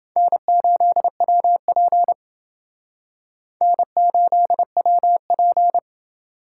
Morse code, heard as a keyed tone, about 28 wpm, 700 Hz. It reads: N8WP N8WP